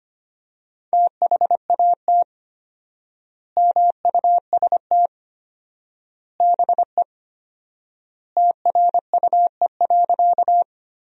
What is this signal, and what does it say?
Morse code, heard as a keyed tone, about 25 wpm, 700 Hz. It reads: THAT MUST BE TRUE.
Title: 4x Word-Spacing - Sentences from Top 400 Words - 25wpm